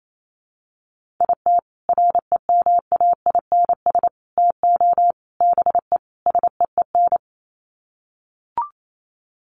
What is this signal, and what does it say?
Morse code, heard as a keyed tone, about 28 wpm, 700 Hz. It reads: IT REMAINS TO BE SEEN
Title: Sets of 5 Words Encore 28wpm